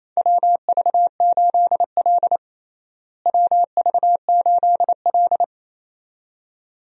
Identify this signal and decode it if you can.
Morse code, heard as a keyed tone, about 28 wpm, 700 Hz. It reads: WV8L WV8L
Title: USA Call Signs - 4 Characters - Repeated 2x - 28wpm